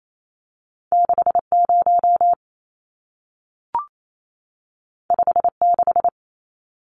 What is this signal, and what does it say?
Morse code, heard as a keyed tone, about 28 wpm, 700 Hz. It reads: 60 56
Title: Numbers - 2-Digit 28wpm